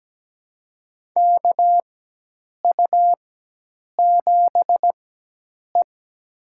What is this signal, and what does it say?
Morse code, heard as a keyed tone, about 17 wpm, spaced as for 8 wpm, 700 Hz. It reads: KU7E
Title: USA Call Signs - 4 Characters - 4x Character Spacing - 17wpm